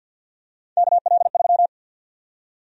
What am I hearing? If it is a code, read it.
Morse code, heard as a keyed tone, about 50 wpm, 700 Hz. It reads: KL3
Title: USA Call Sign Prefixes - 50wpm